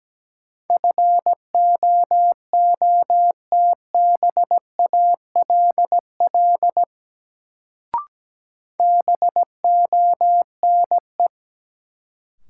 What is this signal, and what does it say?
Morse code, heard as a keyed tone, about 17 wpm, 700 Hz. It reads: FOOTBALL BONE
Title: Sentences from Top 1600 Words - New Word Drill - 17wpm